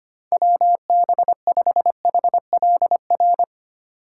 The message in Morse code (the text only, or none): WB5HLR